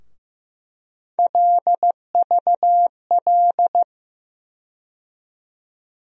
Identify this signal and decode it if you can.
Morse code, heard as a keyed tone, about 15 wpm, 700 Hz. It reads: LVL
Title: USA Call Sign Postfixes - 3 Letters 15wpm